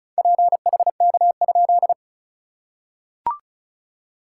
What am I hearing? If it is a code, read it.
Morse code, heard as a keyed tone, about 34 wpm, 690 Hz. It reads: PHK?
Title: SOTA QSO Practice 35wpm